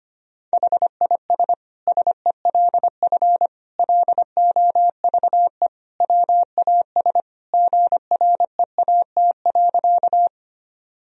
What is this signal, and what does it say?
Morse code, heard as a keyed tone, about 25 wpm, 700 Hz. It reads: HIS SELF LOVE WAS GREAT.
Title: Sentences from Top 1000 Words - Review Part 1 - 25wpm